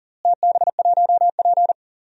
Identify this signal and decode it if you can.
Morse code, heard as a keyed tone, about 40 wpm, 700 Hz. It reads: TB1P
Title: DXCC-100 Call Signs - 4 Characters - Repeated 3x - 40wpm